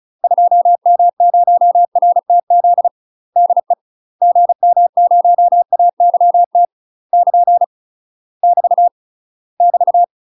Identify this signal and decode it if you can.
Morse code, heard as a keyed tone, about 36 wpm, 700 Hz. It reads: 2M0RTZ DE GM0AYT <KN> <BT> <BT>
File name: UK MASTER QSOs TRAINING / 35WPM / UK MASTER QSOs 0024